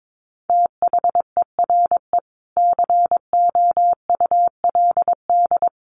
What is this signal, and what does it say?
Morse code, heard as a keyed tone, about 22 wpm, 700 Hz. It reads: THERE COULD
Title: Sentences from Top 600 Words 22wpm